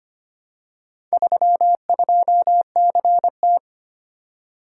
Morse code, heard as a keyed tone, about 25 wpm, 700 Hz. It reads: 32CT
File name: practice / 25wpm / Four Letter-Number Combinations - 25wpm